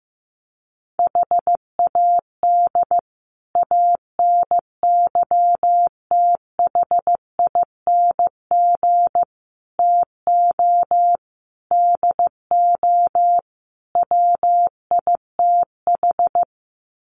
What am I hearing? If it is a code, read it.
Morse code, heard as a keyed tone, about 15 wpm, 700 Hz. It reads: HAD ANYTHING TO DO WITH
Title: Sets of 5 Words Encore 15wpm